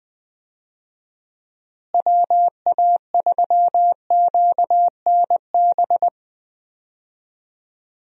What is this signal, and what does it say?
Morse code, heard as a keyed tone, about 20 wpm, 700 Hz. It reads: WA3QNB
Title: USA Call Signs - 6 Characters - Repeated 2x - 20wpm